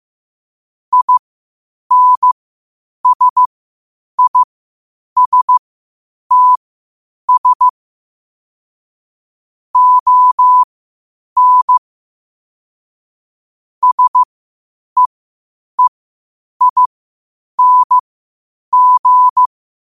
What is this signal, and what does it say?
Morse code, heard as a keyed tone, about 15 wpm, spaced as for 8 wpm, 1000 Hz. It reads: INSISTS ON SEEING
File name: qotdfiles / 8wpm